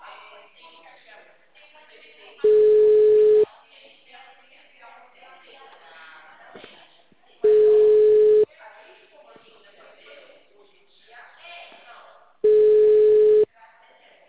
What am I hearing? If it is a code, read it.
Ringback tone (Europe, 425 Hz), with speech.